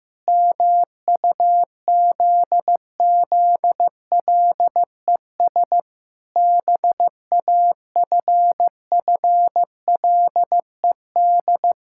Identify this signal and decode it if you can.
Morse code, heard as a keyed tone, about 15 wpm, 700 Hz. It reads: MUZZLES BAFFLED
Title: Pangram Challenge v2 - 15wpm